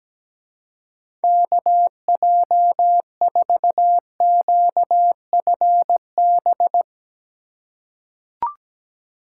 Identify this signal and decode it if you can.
Morse code, heard as a keyed tone, about 17 wpm, 700 Hz. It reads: KJ4QFB
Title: USA Call Signs - 6 Characters - 3x Character Spacing - 17wpm